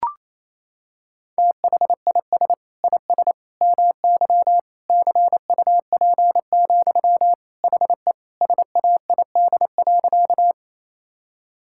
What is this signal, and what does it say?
Morse code, heard as a keyed tone, about 28 wpm, 700 Hz. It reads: THIS IS MY CUP, HE SAID.
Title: Sentences from Top 1600 Words 28wpm